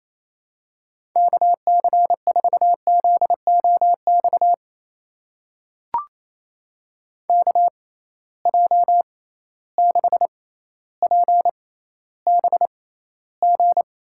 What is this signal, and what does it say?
Morse code, heard as a keyed tone, about 28 wpm, spaced as for 14 wpm, 700 Hz. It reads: KC4ZOX KJ6PBG
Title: USA Call Signs - 6 Characters - 6x Character Spacing - 28wpm